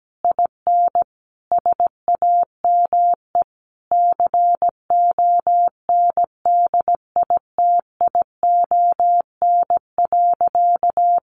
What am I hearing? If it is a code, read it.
Morse code, heard as a keyed tone, about 17 wpm, 700 Hz. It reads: IN SAME CONDITION.
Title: Sentences from Top 1000 Words 17wpm